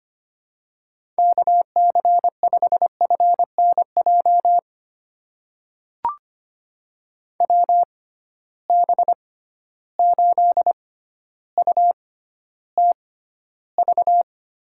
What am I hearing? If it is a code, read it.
Morse code, heard as a keyed tone, about 25 wpm, spaced as for 13 wpm, 700 Hz. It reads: KC5FNJ WB8UTV
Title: USA Call Signs - 6 Characters - 6x Character Spacing - 25wpm